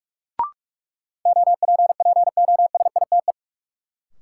Morse code, heard as a keyed tone, about 44 wpm, 700 Hz. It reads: OPPOSITE
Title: Sentences from Top 1300 Words - New Word Drill - 45wpm